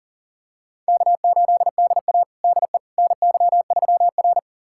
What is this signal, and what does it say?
Morse code, heard as a keyed tone, about 37 wpm, 700 Hz. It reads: K8DA DE NY3R
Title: USA Callsign DE Callsign - 4 Characters - 40wpm